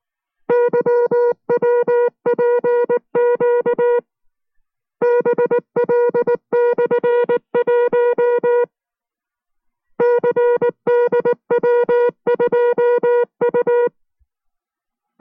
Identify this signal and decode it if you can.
Morse code, heard as a keyed tone, about 19 wpm, 480 Hz. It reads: YWPQ BL/1 CDW2U